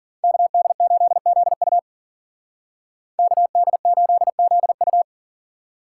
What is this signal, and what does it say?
Morse code, heard as a keyed tone, about 49 wpm, 700 Hz. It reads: KD8ZU KD8ZU
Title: Speed Racing - USA 2x2 Call Signs 40wpm